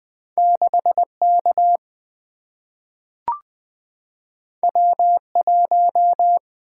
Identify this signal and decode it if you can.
Morse code, heard as a keyed tone, about 20 wpm, 700 Hz. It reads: 6K W1
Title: Single-Letter Single-Number 20wpm